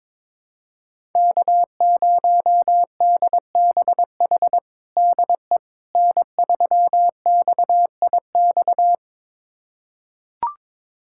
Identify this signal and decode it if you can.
Morse code, heard as a keyed tone, about 22 wpm, 700 Hz. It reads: K0DBH DE N3XIX
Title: USA Callsign DE Callsign - 5 Characters - 22wpm